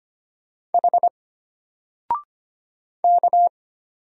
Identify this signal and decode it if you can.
Morse code, heard as a keyed tone, about 25 wpm, 700 Hz. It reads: H K
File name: Single-Letter-Number-25wpm